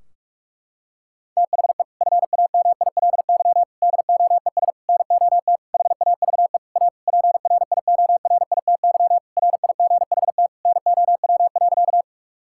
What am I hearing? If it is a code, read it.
Morse code, heard as a keyed tone, about 46 wpm, 700 Hz. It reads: THE FAMILY DOES NOT HAVE A PRIORITY RIGHT NOW.
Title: Sentences from Top 1600 Words 45wpm